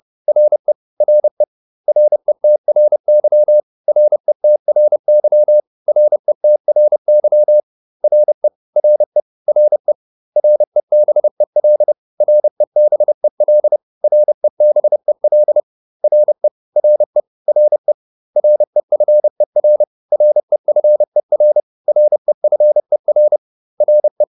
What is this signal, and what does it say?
Morse code, heard as a keyed tone, about 30 wpm, 600 Hz. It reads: RE RE RETRY RETRY RETRY RE RE RE REBEL REBEL REBEL RE RE RE REFER REFER REFER RE